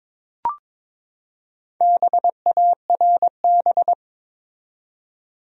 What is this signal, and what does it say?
Morse code, heard as a keyed tone, about 22 wpm, 700 Hz. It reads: BARB
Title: USA Common Names - Max 5 Letters - 22wpm